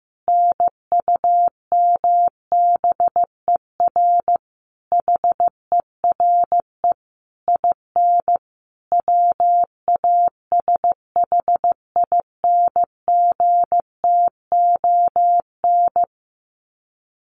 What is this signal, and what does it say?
Morse code, heard as a keyed tone, about 15 wpm, 700 Hz. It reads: NUMBER HERE IN WASHINGTON
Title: Sets of 5 Words Encore2 15wpm